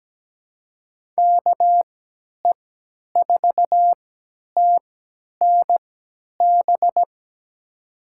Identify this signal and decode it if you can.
Morse code, heard as a keyed tone, about 17 wpm, spaced as for 10 wpm, 700 Hz. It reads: KE4TNB
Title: USA Call Signs - 3x Character Spacing - 17wpm